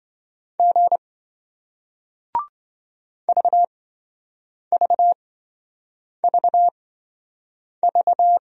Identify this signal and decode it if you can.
Morse code, heard as a keyed tone, about 27 wpm, spaced as for 7 wpm, 700 Hz. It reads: G VVVV